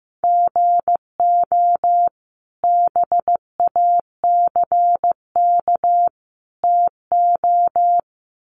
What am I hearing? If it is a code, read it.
Morse code, heard as a keyed tone, about 15 wpm, 700 Hz. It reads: GO BACK TO
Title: Sentences from Top 500 Words 15wpm